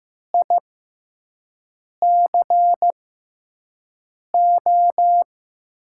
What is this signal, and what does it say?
Morse code, heard as a keyed tone, about 15 wpm, spaced as for 5 wpm, 700 Hz. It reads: ICO
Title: USA Call Signs - 6x Character Spacing - 15wpm